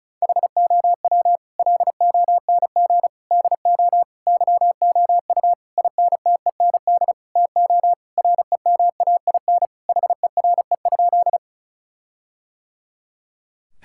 Morse code, heard as a keyed tone, about 36 wpm, 700 Hz. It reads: HOW LONG DO YOU INTEND TO REMAIN HERE?
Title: Sentences from Top 1600 Words 35wpm